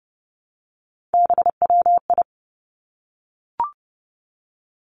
Morse code, heard as a keyed tone, about 30 wpm, 700 Hz. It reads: BWI
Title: Three Challenging Letter-Number Combinations 30wpm